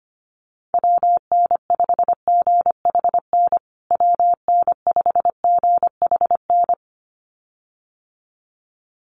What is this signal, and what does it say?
Morse code, heard as a keyed tone, about 25 wpm, 700 Hz. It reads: WN5GHN WN5GHN
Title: USA Call Signs Repeated 2x 25wpm